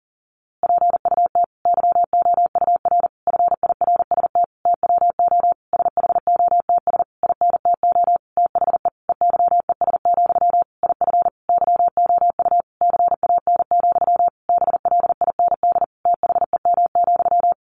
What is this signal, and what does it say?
Morse code, heard as a keyed tone, about 40 wpm, 700 Hz. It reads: PUT YOUR FIRST TWO SHOTS INTO THE EYES, IF YOU CAN, BLIND THEM,
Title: A Sound Of Thunder 40wpm